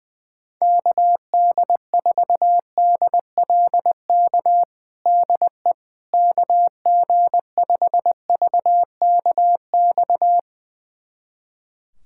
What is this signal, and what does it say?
Morse code, heard as a keyed tone, about 21 wpm, 700 Hz. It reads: KD4DLK DE KG5VKX
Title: USA Callsign DE Callsign - 6 Characters - 20wpm